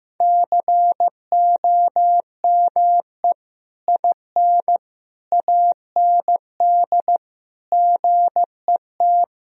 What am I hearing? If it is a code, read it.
Morse code, heard as a keyed tone, about 15 wpm, 700 Hz. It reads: COME IN AND GET
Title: Sentences from Top 1000 Words - Review Part 2 - 15wpm